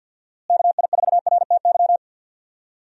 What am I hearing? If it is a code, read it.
Morse code, heard as a keyed tone, about 47 wpm, 700 Hz. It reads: KI4RTY
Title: USA Call Signs - 6 Characters - Repeated 2x - 50wpm